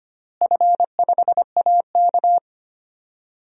Morse code, heard as a keyed tone, about 25 wpm, 700 Hz. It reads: F5AK
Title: DXCC-100 Call Signs - 4 Characters - Repeated 2x - 25wpm